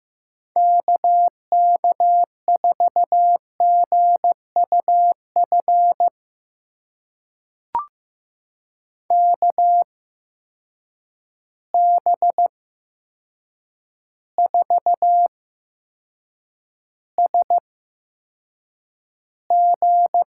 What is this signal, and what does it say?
Morse code, heard as a keyed tone, about 15 wpm, spaced as for 7 wpm, 700 Hz. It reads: KK4GUF KB4SG
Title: USA Call Signs - 6 Characters - 8x Character Spacing - 15wpm